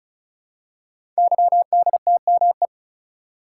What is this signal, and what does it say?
Morse code, heard as a keyed tone, about 36 wpm, 700 Hz. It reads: YDTME
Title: Code Groups - 5 Letters - 35wpm